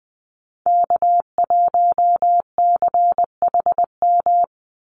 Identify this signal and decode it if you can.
Morse code, heard as a keyed tone, about 20 wpm, 700 Hz. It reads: K1CHM
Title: Call Signs Repeated 3x - 20wpm